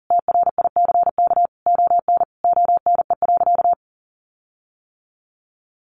Morse code, heard as a keyed tone, about 40 wpm, 700 Hz. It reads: TRICK ON ONE.
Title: Sentences from Top 1500 Words 40wpm